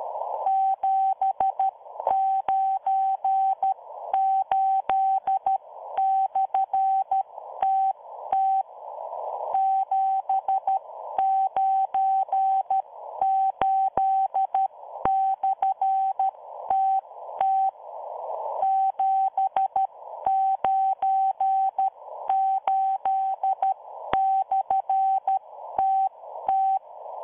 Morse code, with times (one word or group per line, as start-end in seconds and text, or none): 0.5-8.6 798/TT
9.5-17.7 798/TT
18.6-26.8 798/TT